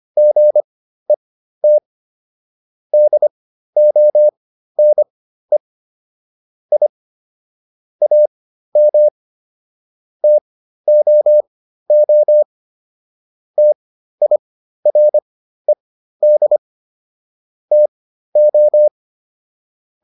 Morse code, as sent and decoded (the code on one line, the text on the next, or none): --. - / -.. --- -. / .. / .- -- / - --- --- / - .. .-. -.. / - ---
GET DONE I AM TOO TIRED TO